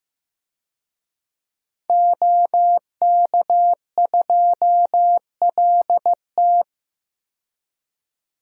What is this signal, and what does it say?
Morse code, heard as a keyed tone, about 15 wpm, 700 Hz. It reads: OK2LT
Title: DXCC-100 Call Signs - 5 Characters - 15wpm